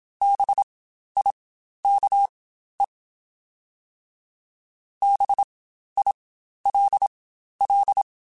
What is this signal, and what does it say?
Morse code, heard as a keyed tone, about 27 wpm, spaced as for 12 wpm, 800 Hz. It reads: BIKE BILL
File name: Spaced Alphas / Spaced Alphas mp3s / Bs sp @25